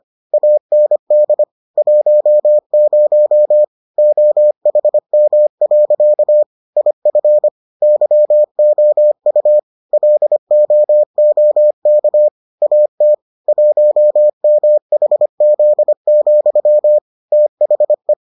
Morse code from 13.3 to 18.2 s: 1MHZ, THE

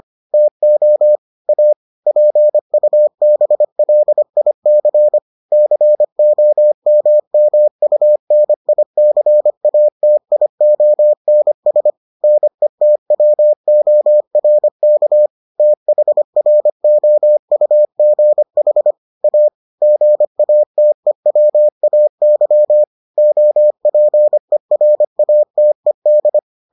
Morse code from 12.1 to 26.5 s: NETWORK THROUGH A GATEWAY OPERATED

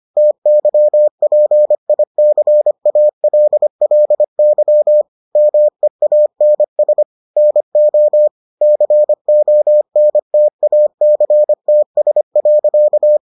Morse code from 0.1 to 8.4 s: TYPICALLY MEANS NO

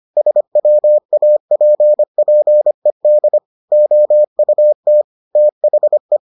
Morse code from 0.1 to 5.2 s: SWAPPED OUT

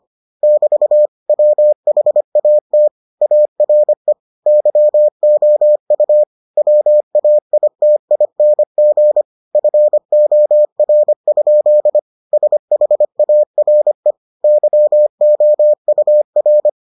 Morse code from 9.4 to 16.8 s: FOR? SHARE YOUR

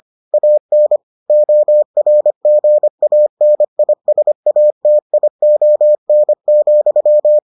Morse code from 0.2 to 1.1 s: AN

